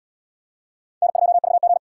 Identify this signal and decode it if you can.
Morse code, heard as a keyed tone, about 67 wpm, 700 Hz. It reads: N2VG